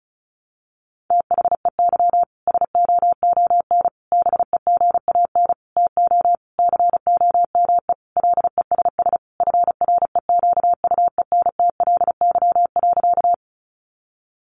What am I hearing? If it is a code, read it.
Morse code, heard as a keyed tone, about 34 wpm, 700 Hz. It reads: THEY SOON BEGAN TO COME LESS FREQUENTLY.
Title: Sentences from Top 1300 Words 35wpm